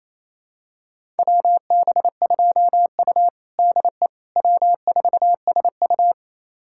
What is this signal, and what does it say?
Morse code, heard as a keyed tone, about 28 wpm, 700 Hz. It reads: WB2U DE W4SU
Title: USA Callsign DE Callsign - 4 Characters - 28wpm